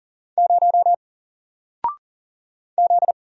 Morse code, heard as a keyed tone, about 39 wpm, 700 Hz. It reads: 0 Z
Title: Single Letter-Number 40wpm